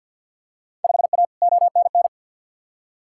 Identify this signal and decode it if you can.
Morse code, heard as a keyed tone, about 52 wpm, 700 Hz. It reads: 5A ONN
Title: Field Day Exchanges Encore - 50wpm